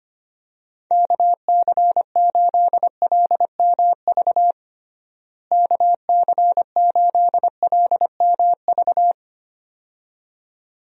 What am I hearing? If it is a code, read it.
Morse code, heard as a keyed tone, about 25 wpm, 700 Hz. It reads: KC8LMV KC8LMV
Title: USA Call Signs - 6 Characters - Repeated 2x - 25wpm